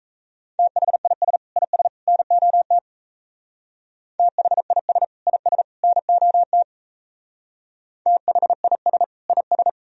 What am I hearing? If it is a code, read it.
Morse code, heard as a keyed tone, about 35 wpm, 700 Hz. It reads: THIS IS NOT THIS IS NOT THIS IS